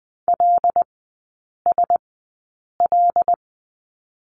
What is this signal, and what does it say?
Morse code, heard as a keyed tone, about 20 wpm, spaced as for 8 wpm, 700 Hz. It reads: LSL
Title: 074 Lesson 13e - Letter L - Three Characters 20wpm